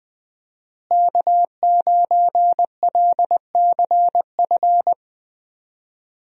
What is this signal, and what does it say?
Morse code, heard as a keyed tone, about 20 wpm, 700 Hz. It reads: K9LCF
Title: USA Call Signs - 5 Characters - Repeated 2x - 20wpm